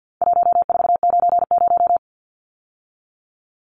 Morse code, heard as a keyed tone, about 50 wpm, 700 Hz. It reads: 1490
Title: Numbers - 4-Digit 50wpm